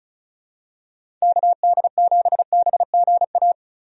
Morse code, heard as a keyed tone, about 34 wpm, 700 Hz. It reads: KD7BGA
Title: USA Call Signs - 6 Characters - 35wpm